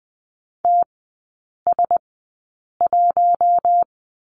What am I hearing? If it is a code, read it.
Morse code, heard as a keyed tone, about 20 wpm, spaced as for 8 wpm, 700 Hz. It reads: TS1
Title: 041 Lesson 8e - Number 1 - Three Characters 20wpm